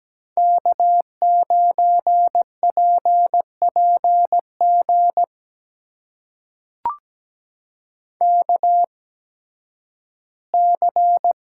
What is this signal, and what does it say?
Morse code, heard as a keyed tone, about 17 wpm, spaced as for 11 wpm, 700 Hz. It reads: K9PPG KC
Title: USA Call Signs - 8x Character Spacing - 17wpm